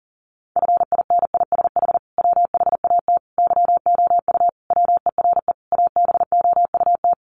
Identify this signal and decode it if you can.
Morse code, heard as a keyed tone, about 40 wpm, 700 Hz. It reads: FINISH WHAT YOU WERE ABOUT
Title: Sentences from Top 1200 Words 40wpm